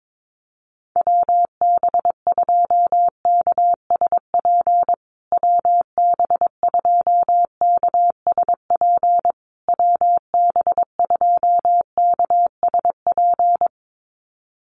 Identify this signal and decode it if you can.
Morse code, heard as a keyed tone, about 22 wpm, 700 Hz. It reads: WB2KSP WB2KSP WB2KSP
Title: USA Call Signs Repeated 3x 22wpm